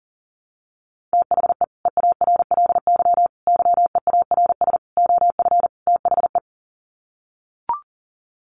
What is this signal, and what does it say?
Morse code, heard as a keyed tone, about 40 wpm, 700 Hz. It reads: THE EARLY YEARS OF THE
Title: Sets of 5 Words Encore 40wpm